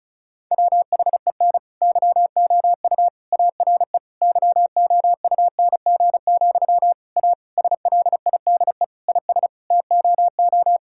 Morse code, heard as a keyed tone, about 34 wpm, 700 Hz. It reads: WHEN YOU ARE YOUNG, A SLIDE IS TOO
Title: Sentences from Top 1600 Words 35wpm